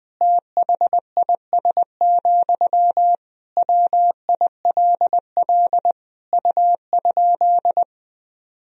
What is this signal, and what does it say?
Morse code, heard as a keyed tone, about 20 wpm, 700 Hz. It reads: THIS, WILL U?
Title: Sentences from Top 400 Words and Abbreviations 20wpm